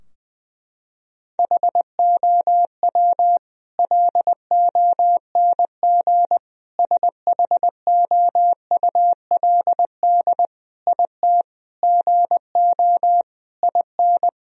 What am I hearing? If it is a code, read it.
Morse code, heard as a keyed tone, about 20 wpm, 700 Hz. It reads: HOW LONG SHOULD IT GO IN
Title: Sentences from Top 1000 Words - Review Part 4 - 20wpm